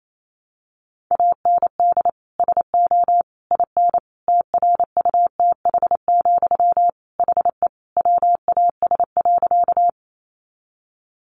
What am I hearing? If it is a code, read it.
Morse code, heard as a keyed tone, about 28 wpm, 700 Hz. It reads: AND SO IN TRUTH, HE WAS.
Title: Sentences from Top 800 Words 28wpm